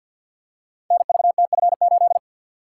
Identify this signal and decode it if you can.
Morse code, heard as a keyed tone, about 46 wpm, 700 Hz. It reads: NVTF8
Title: Code Groups - 5 Letter:Numbers - 50wpm